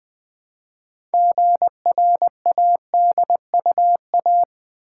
Morse code, heard as a keyed tone, about 20 wpm, 700 Hz. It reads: GRADUA
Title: Build A Word - 2 Letters At A Time - 8-Letter Words - 20wpm